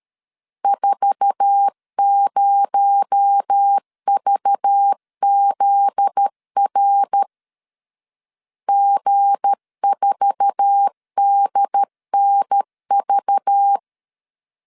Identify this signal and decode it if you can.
Morse code, heard as a keyed tone, about 13 wpm, 800 Hz. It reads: 40VZR G4DNV
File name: LEZIONI / 8 YJ45 / 8 YJ4 12wpm 2151